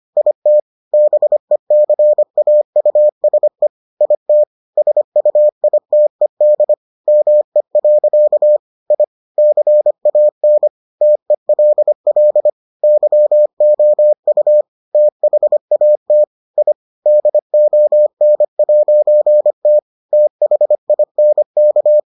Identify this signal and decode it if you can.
Morse code, heard as a keyed tone, about 25 wpm, 600 Hz. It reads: IT BECAUSE IT SUITED ME. I CAN TELL YOU THAT I DON'T THINK